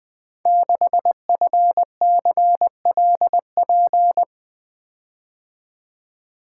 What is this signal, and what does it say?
Morse code, heard as a keyed tone, about 20 wpm, 700 Hz. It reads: 6FCLP
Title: Code Groups - 5 Letter:Numbers - 20wpm